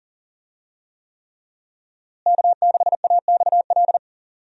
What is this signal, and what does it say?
Morse code, heard as a keyed tone, about 39 wpm, 700 Hz. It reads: K6AXL
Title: USA Call Signs - 5 Characters - Repeated 3x - 40wpm